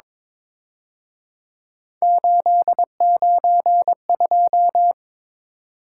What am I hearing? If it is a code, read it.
Morse code, heard as a keyed tone, about 26 wpm, 700 Hz. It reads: 892
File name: Speed-Racing - Three Letter-Number Combinations - 22wpm